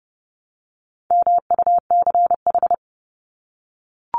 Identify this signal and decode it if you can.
Morse code, heard as a keyed tone, about 30 wpm, 700 Hz. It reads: MUCH
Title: Top 200 Words 30wpm